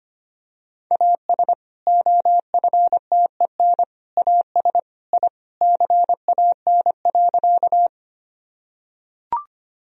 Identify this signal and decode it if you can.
Morse code, heard as a keyed tone, about 25 wpm, 700 Hz. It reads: AS OFTEN AS I CAN.
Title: Sentences from Top 1000 Words - Review Part 1 - 25wpm